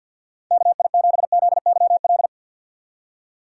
Morse code, heard as a keyed tone, about 52 wpm, 700 Hz. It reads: KI7ZYL
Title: USA Call Signs - 6 Characters - 4x Character Spacing - 50wpm